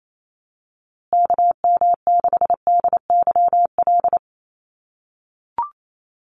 Morse code, heard as a keyed tone, about 28 wpm, 700 Hz. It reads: KM6DYL
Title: USA Call Signs Repeated 3x 28wpm